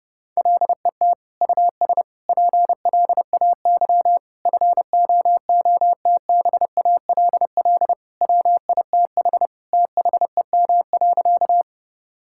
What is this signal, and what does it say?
Morse code, heard as a keyed tone, about 30 wpm, 700 Hz. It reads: LET US PLAY FOOTBALL WITH THEM.
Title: Sentences from Top 1600 Words 30wpm